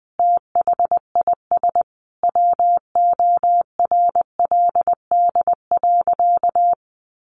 Morse code, heard as a keyed tone, about 20 wpm, 700 Hz. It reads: THIS WORLD.
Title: Sentences from Top 600 Words 20wpm